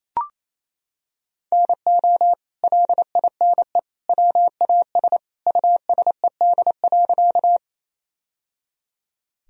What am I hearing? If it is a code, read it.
Morse code, heard as a keyed tone, about 28 wpm, 700 Hz. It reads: NO LINE WAS USED.
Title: Sentences from Top 1000 Words - Review Part 1 - 28wpm